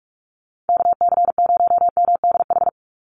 Morse code, heard as a keyed tone, about 44 wpm, 700 Hz. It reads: KC0MDH
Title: USA Call Signs 45wpm